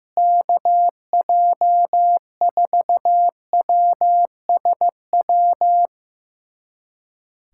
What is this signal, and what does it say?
Morse code, heard as a keyed tone, about 15 wpm, 700 Hz. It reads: KJ4WSW
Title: USA Callsign DE Callsign - 6 Characters - 15wpm